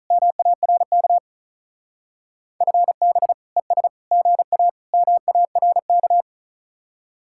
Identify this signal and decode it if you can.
Morse code, heard as a keyed tone, about 45 wpm, 700 Hz. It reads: MARK FB ES GA MARK